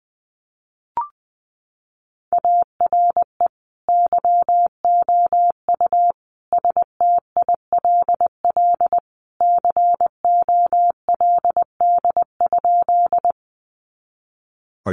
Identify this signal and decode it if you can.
Morse code, heard as a keyed tone, about 20 wpm, 700 Hz. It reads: ARE YOU STILL COLD?
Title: Sentences from Top 400 Words 20wpm